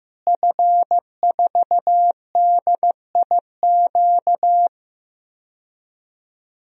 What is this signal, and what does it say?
Morse code, heard as a keyed tone, about 15 wpm, 700 Hz. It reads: F4DIQ
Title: DXCC-100 Call Signs - 5 Characters - Repeated 3x - 15wpm